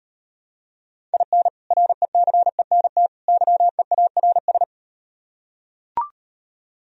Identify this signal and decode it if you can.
Morse code, heard as a keyed tone, about 38 wpm, 700 Hz. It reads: IN RECENT YEARS